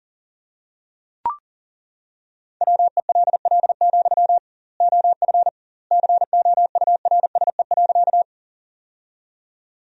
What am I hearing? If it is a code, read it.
Morse code, heard as a keyed tone, about 40 wpm, 700 Hz. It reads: WELL, OF COURSE.
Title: 2x Word-Spacing - Sentences from Top 500 Words - 40wpm